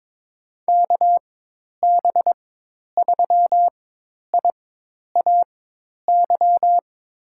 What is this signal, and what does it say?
Morse code, heard as a keyed tone, about 23 wpm, spaced as for 10 wpm, 700 Hz. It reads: KB3IAY
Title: USA Call Signs - 6 Characters - 4x Character Spacing - 22wpm